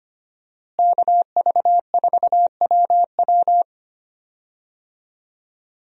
Morse code, heard as a keyed tone, about 25 wpm, 700 Hz. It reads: KV4WW
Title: USA Call Signs - 5 Characters - Repeated 2x - 25wpm